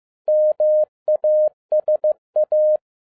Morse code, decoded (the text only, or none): MASA